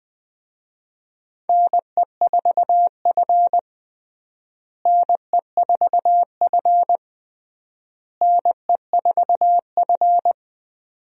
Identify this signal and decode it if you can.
Morse code, heard as a keyed tone, about 20 wpm, 700 Hz. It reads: NE4F NE4F NE4F
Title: USA Call Signs - 4 Characters - Repeated 3x - 20wpm